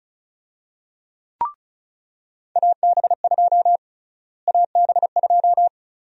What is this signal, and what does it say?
Morse code, heard as a keyed tone, about 36 wpm, 700 Hz. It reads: AB2 AB2
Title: USA Call Sign Prefixes - Repeated 2x - 35wpm